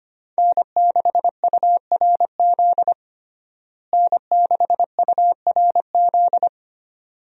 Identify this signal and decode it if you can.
Morse code, heard as a keyed tone, about 25 wpm, 700 Hz. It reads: N6URZ N6URZ
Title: USA Call Signs - 5 Characters - Repeated 2x - 25wpm